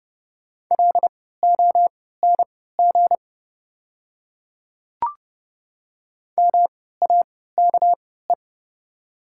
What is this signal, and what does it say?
Morse code, heard as a keyed tone, about 30 wpm, spaced as for 17 wpm, 700 Hz. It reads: LONG MAKE
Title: 3x Character Spacing - Top 100 Words - 30wpm